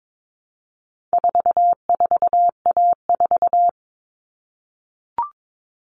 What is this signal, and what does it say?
Morse code, heard as a keyed tone, about 22 wpm, 700 Hz. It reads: 44A4
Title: All Cut Numbers - 4-Digit 22wpm